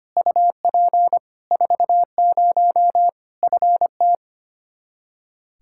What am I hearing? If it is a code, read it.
Morse code, heard as a keyed tone, about 25 wpm, 700 Hz. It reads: UP 40 FT
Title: Practice QSO - Exchange 2 - Info - 25wpm